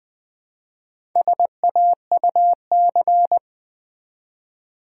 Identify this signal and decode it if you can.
Morse code, heard as a keyed tone, about 20 wpm, 700 Hz. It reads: SAUC
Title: Build A Word - 1 Letter At A Time - 5-Letter Words - 20wpm